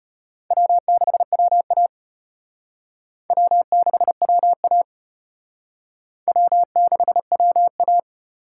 Morse code, heard as a keyed tone, about 34 wpm, 700 Hz. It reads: W6WA W6WA W6WA